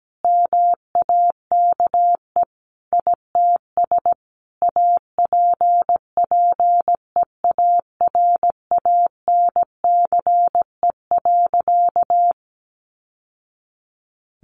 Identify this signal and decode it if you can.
Morse code, heard as a keyed tone, about 17 wpm, 700 Hz. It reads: MAKE ITS APPEARANCE.
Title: Sentences from Top 1100 Words 17wpm